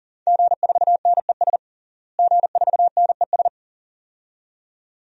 Morse code, heard as a keyed tone, about 40 wpm, 700 Hz. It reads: G4NES G4NES
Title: DXCC-100 Call Signs - 5 Characters - Repeated 2x - 40wpm